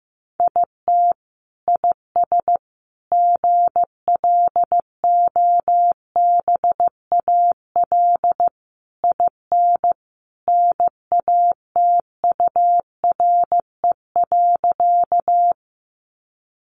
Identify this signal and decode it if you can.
Morse code, heard as a keyed tone, about 15 wpm, 700 Hz. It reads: IT IS GLOBAL IN NATURE.
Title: Sentences from Top 1200 Words 15wpm